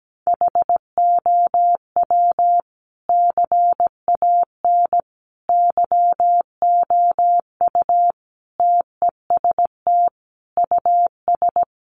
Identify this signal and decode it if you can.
Morse code, heard as a keyed tone, about 17 wpm, 700 Hz. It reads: HOW CAN YOU TEST US
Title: Sentences from Top 1100 Words 17wpm